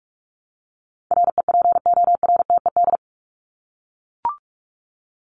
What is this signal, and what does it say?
Morse code, heard as a keyed tone, about 44 wpm, 700 Hz. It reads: REPORTED
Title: ED Suffix - Top 100 Words with Base Words 45wpm